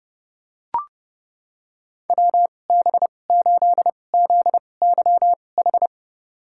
Morse code, heard as a keyed tone, about 31 wpm, spaced as for 22 wpm, 700 Hz. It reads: WB8ZYH
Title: USA Call Signs - 2x Character Spacing - 30wpm